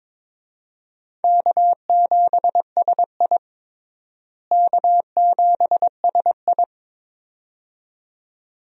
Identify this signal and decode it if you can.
Morse code, heard as a keyed tone, about 22 wpm, 700 Hz. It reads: K7SI K7SI
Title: USA Call Signs - 4 Characters - Repeated 2x - 22wpm